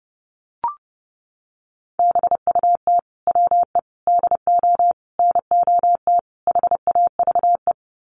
Morse code, heard as a keyed tone, about 30 wpm, 700 Hz. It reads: BUT WE DO NOT HAVE
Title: Sets of 5 Words Encore2 30wpm